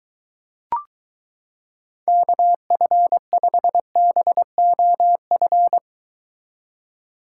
Morse code, heard as a keyed tone, about 23 wpm, 700 Hz. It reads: KF5BOF